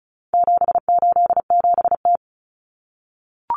Morse code, heard as a keyed tone, about 34 wpm, 700 Hz. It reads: 787T
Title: Common Cut Numbers - 4-Digit 35wpm